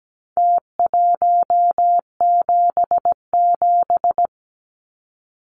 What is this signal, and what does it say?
Morse code, heard as a keyed tone, about 17 wpm, 700 Hz. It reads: T177
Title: Numbers - Contest Style - 4-Digit 17wpm